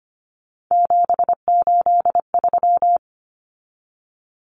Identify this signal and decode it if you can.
Morse code, heard as a keyed tone, about 25 wpm, 700 Hz. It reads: 783
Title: Numbers - 3-Digit 25wpm